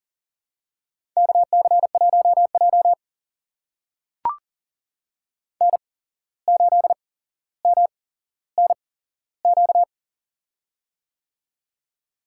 Morse code, heard as a keyed tone, about 40 wpm, spaced as for 16 wpm, 700 Hz. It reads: KC1J N8MNQ
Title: USA Call Signs - 8x Character Spacing - 40wpm